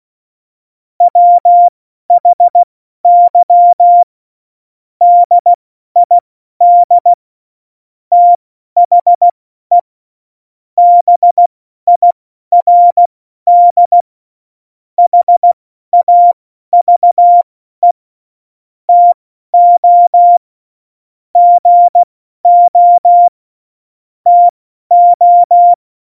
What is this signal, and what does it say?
Morse code, heard as a keyed tone, about 16 wpm, spaced as for 12 wpm, 700 Hz. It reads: WHY DID THE BIRD HAVE TO GO TO
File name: Morsekurs / Witze